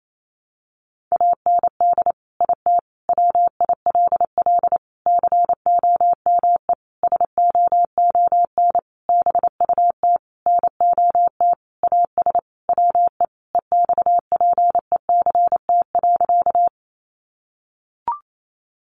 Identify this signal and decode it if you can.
Morse code, heard as a keyed tone, about 28 wpm, 700 Hz. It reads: AND IT WILL COME SOON BUT NOT AS WE EXPECT.
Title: Sentences from Top 1100 Words 28wpm